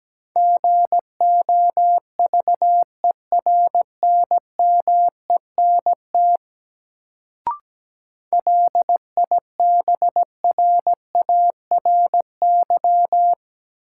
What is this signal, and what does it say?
Morse code, heard as a keyed tone, about 17 wpm, 700 Hz. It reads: GOVERNMENT LIBRARY
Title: Sentences from Top 600 Words - New Word Drill - 17wpm